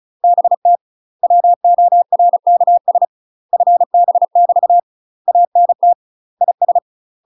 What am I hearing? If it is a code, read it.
Morse code, heard as a keyed tone, about 33 wpm, 700 Hz. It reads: BT WORKS FB <BT> ANT IS
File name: UK MASTER QSOs TRAINING / 35WPM / UK MASTER QSOs 0035